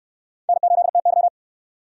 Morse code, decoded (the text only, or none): N8TY